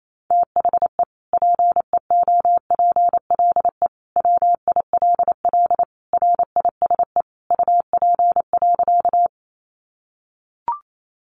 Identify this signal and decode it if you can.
Morse code, heard as a keyed tone, about 28 wpm, 700 Hz. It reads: THE PEOPLE WILL RISE UP.
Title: Sentences from Top 900 Words 28wpm